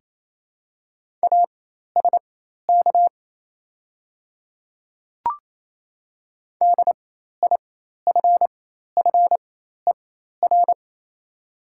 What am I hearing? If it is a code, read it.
Morse code, heard as a keyed tone, about 28 wpm, spaced as for 13 wpm, 690 Hz. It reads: ASK DIFFER
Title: 4x Character Spacing - Top 200 Words - 28wpm